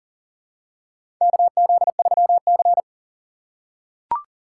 Morse code, heard as a keyed tone, about 40 wpm, 700 Hz. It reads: KZ3C
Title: USA Call Signs - 4 Characters - 8x Character Spacing - 40wpm